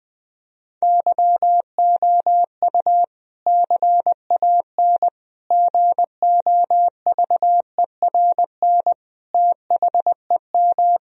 Morse code, heard as a keyed tone, about 20 wpm, 700 Hz. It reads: YOU CAN GOVERN THEM
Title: Sentences from Top 1000 Words - Review Part 2 - 20wpm